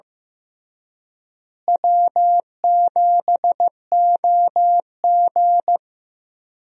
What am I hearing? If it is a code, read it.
Morse code, heard as a keyed tone, about 15 wpm, 700 Hz. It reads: W7OG
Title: USA Call Signs - 4 Characters - Repeated 3x - 15wpm